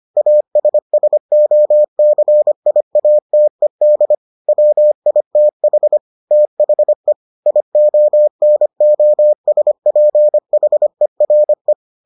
Morse code, heard as a keyed tone, about 25 wpm, 600 Hz. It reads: ASSOCIATED WITH THE IONOSPHERE